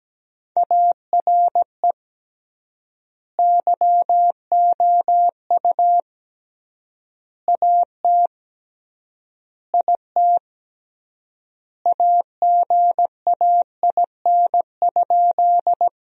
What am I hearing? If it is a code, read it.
Morse code, heard as a keyed tone, about 17 wpm, 700 Hz. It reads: ARE YOU AT IT AGAIN?
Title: 3x Word-Spacing - Sentences from Top 200 Words - 17wpm